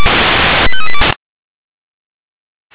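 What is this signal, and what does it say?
Five-tone selective call (ZVEI), heard as quick runs of five tones, tones of about 80 ms. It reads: 85382
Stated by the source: ZVEI five-tone selective call received by SDR.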